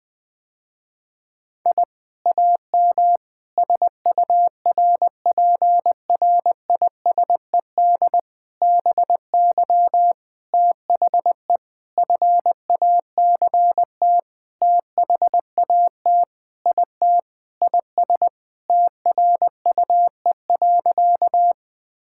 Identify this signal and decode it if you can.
Morse code, heard as a keyed tone, about 20 wpm, 700 Hz. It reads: I AM SURPRISED BY THE FACT THAT IT IS TRUE.
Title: Sentences with 4-Word Phrases - Spoken Beginning - 20wpm